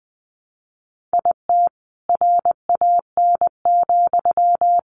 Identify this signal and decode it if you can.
Morse code, heard as a keyed tone, about 20 wpm, 700 Hz. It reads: IT RAN,
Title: A Sound Of Thunder 20wpm